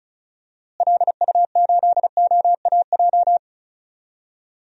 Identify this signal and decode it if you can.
Morse code, heard as a keyed tone, about 36 wpm, 700 Hz. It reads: LU8OAJ
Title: DXCC-100 Call Signs - 6 Characters - Repeated 2x - 35wpm